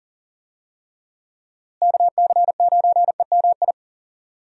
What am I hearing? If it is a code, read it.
Morse code, heard as a keyed tone, about 40 wpm, 700 Hz. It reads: KC9EMI
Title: USA Call Signs - 6 Characters - Repeated 2x - 40wpm